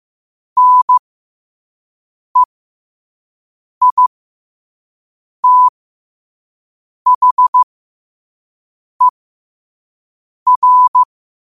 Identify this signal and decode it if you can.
Morse code, heard as a keyed tone, about 15 wpm, spaced as for 5 wpm, 1000 Hz. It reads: NEITHER
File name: qotdfiles / 5wpm